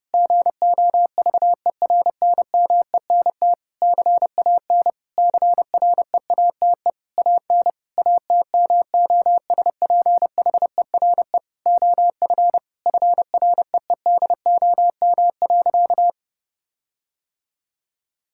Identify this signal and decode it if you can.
Morse code, heard as a keyed tone, about 30 wpm, 700 Hz. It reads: GOVERNMENT CAN CREATE AN ATMOSPHERE OF FREEDOM.
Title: Sentences from Top 1600 Words 30wpm